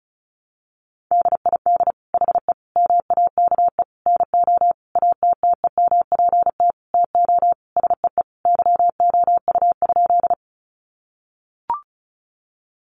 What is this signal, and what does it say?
Morse code, heard as a keyed tone, about 34 wpm, 700 Hz. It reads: DID HE MAKE NO ATTEMPT TO SEE YOU?
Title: Sentences from Top 1000 Words 35wpm